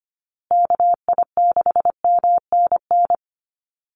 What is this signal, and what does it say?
Morse code, heard as a keyed tone, about 25 wpm, 700 Hz. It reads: KI6MNN
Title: USA Call Signs Repeated 2x 25wpm